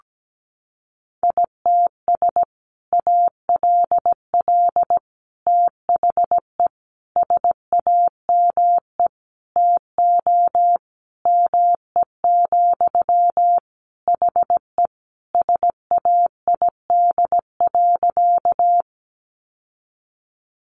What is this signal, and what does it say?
Morse code, heard as a keyed tone, about 17 wpm, 700 Hz. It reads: ITS ALL THE SAME TO ME, HE SAID.
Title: Sentences from Top 600 Words 17wpm